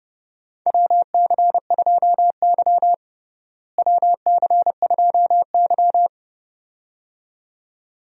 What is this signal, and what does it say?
Morse code, heard as a keyed tone, about 30 wpm, 700 Hz. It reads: WC2Y WC2Y
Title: USA Call Signs - 4 Characters - Repeated 2x - 30wpm